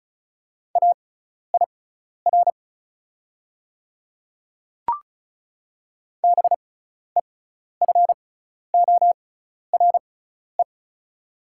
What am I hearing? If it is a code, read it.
Morse code, heard as a keyed tone, about 34 wpm, spaced as for 12 wpm, 690 Hz. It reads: AIR BEFORE
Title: 6x Character Spacing - Top 200 Words - 35wpm